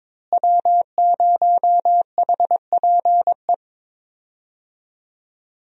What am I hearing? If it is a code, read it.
Morse code, heard as a keyed tone, about 22 wpm, 700 Hz. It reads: W0HPE
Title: USA Call Signs - 5 Characters - Repeated 3x - 22wpm